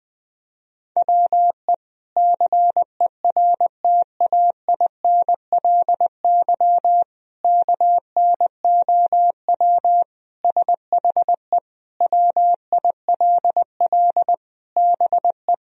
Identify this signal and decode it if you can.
Morse code, heard as a keyed tone, about 20 wpm, 700 Hz. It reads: WE CERTAINLY KNOW SHE WILL BE
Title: Sentences from Top 1000 Words - Review Part 4 - 20wpm